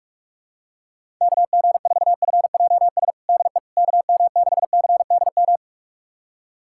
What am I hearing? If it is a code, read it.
Morse code, heard as a keyed tone, about 46 wpm, 700 Hz. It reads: KG4FJS DE KM6CDM
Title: USA Callsign DE Callsign 45wpm